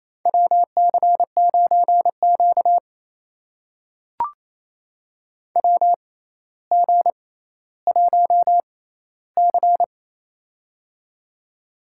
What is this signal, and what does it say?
Morse code, heard as a keyed tone, about 28 wpm, spaced as for 14 wpm, 700 Hz. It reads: WC9Q WG1C
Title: USA Call Signs - 4 Characters - 6x Character Spacing - 28wpm